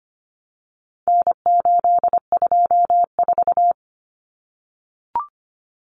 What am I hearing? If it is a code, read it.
Morse code, heard as a keyed tone, about 25 wpm, 700 Hz. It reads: N824
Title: Common Cut Numbers - 4-Digit 25wpm